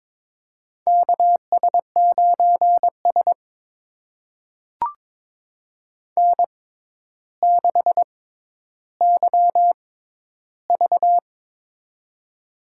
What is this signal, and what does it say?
Morse code, heard as a keyed tone, about 22 wpm, spaced as for 11 wpm, 700 Hz. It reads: KS9S N6YV
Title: USA Call Signs - 4 Characters - 6x Character Spacing - 22wpm